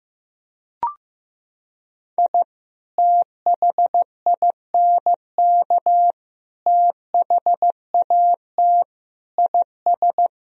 Morse code, heard as a keyed tone, about 15 wpm, 700 Hz. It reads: I THINK THAT IS
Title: Sentences from Top 1000 Words - Review Part 4 - 15wpm